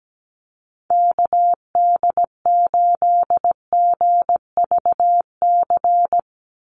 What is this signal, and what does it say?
Morse code, heard as a keyed tone, about 17 wpm, 700 Hz. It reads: KD8GVC
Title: USA Call Signs 17wpm